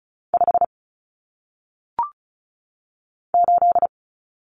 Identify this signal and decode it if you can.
Morse code, heard as a keyed tone, about 36 wpm, 700 Hz. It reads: 5 8
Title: Numbers - 1-Digit 35wpm